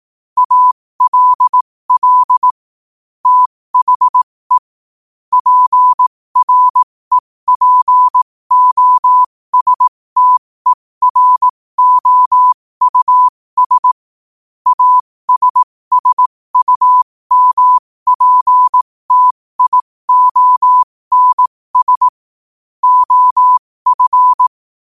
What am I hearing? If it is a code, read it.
Morse code, heard as a keyed tone, about 18 wpm, spaced as for 15 wpm, 1000 Hz. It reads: ALL THE PREPOSTEROUS ASSUMPTIONS OF
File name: qotdfiles / 15wpm